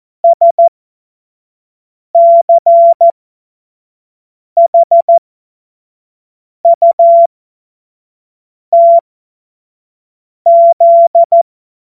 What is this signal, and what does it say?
Morse code, heard as a keyed tone, about 14 wpm, spaced as for 5 wpm, 680 Hz. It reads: SCHUTZ
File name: Morsekurs / Witze / Witz 17. August 2021, 25 F70